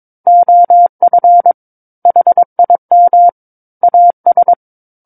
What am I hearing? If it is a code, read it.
Morse code, heard as a keyed tone, about 22 wpm, 700 Hz. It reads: OF HIM AS
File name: Book One / 13-Chapter 13 - 22 WPM